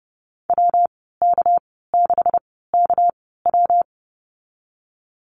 Morse code, heard as a keyed tone, about 30 wpm, spaced as for 17 wpm, 700 Hz. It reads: WK6KW
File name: Lesson 37g - Letter x - Call Signs 30wpm